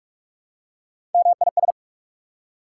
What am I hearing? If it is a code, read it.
Morse code, heard as a keyed tone, about 44 wpm, 690 Hz. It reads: MIS
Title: Prefixes and Suffixes 45wpm